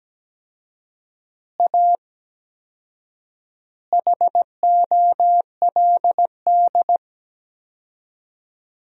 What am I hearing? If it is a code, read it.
Morse code, heard as a keyed tone, about 17 wpm, 700 Hz. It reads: A HOLD
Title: 4x Word-Spacing - Sentences from Top 400 Words - 17wpm